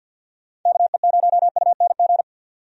Morse code, heard as a keyed tone, about 49 wpm, 700 Hz. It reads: KE0UNG
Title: USA Call Signs - 6 Characters - 50wpm